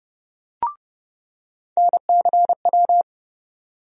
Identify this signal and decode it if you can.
Morse code, heard as a keyed tone, about 30 wpm, 700 Hz. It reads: NCW
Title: USA Call Sign Postfixes - 3 Letters 30wpm